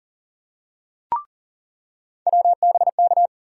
Morse code, heard as a keyed tone, about 40 wpm, 700 Hz. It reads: WBK
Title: USA Call Sign Postfixes - 3 Letters 40wpm